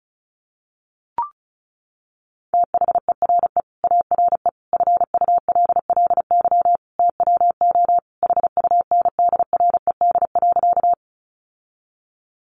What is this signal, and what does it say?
Morse code, heard as a keyed tone, about 34 wpm, 700 Hz. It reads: THERE ARE FULLY TWO HUNDRED.
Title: Sentences from Top 1300 Words 35wpm